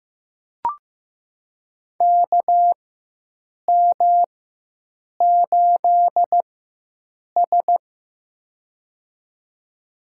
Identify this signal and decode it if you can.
Morse code, heard as a keyed tone, about 15 wpm, spaced as for 7 wpm, 700 Hz. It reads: KM8S
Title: USA Call Signs - 4 Characters - 4x Character Spacing - 15wpm